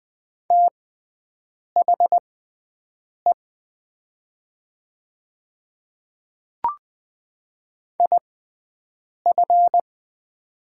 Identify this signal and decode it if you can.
Morse code, heard as a keyed tone, about 20 wpm, spaced as for 7 wpm, 700 Hz. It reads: THE IF
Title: 6x Character Spacing - Top 100 Words - 20wpm